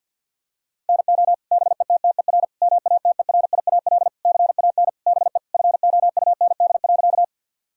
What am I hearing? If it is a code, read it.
Morse code, heard as a keyed tone, about 49 wpm, 700 Hz. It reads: NO BETTER MATERIAL CAN BE FOUND.
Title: Sentences from Top 1000 Words - Review Part 5 - 50wpm